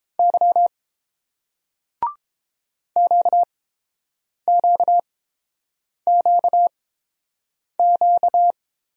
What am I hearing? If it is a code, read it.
Morse code, heard as a keyed tone, about 30 wpm, spaced as for 7 wpm, 700 Hz. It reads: Y QQQQ